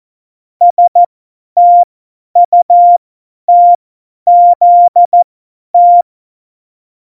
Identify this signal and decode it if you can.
Morse code, heard as a keyed tone, about 14 wpm, spaced as for 10 wpm, 700 Hz. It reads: STUTZT
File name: Morsekurs / Witze / Witz 11. Februar 2020, 50 F70